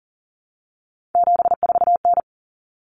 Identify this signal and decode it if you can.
Morse code, heard as a keyed tone, about 40 wpm, 700 Hz. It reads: 74N
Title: Common Cut Numbers - 3-Digit 40wpm